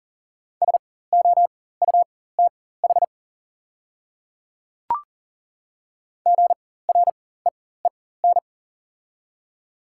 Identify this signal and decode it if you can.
Morse code, heard as a keyed tone, about 40 wpm, spaced as for 19 wpm, 700 Hz. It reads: SOUTH GREEN
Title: 4x Character Spacing - Top 500 Words - 40wpm